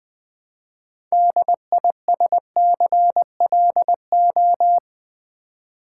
Morse code, heard as a keyed tone, about 20 wpm, 700 Hz. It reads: DISCLO
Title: Build A Word - 2 Letters At A Time - 8-Letter Words - 20wpm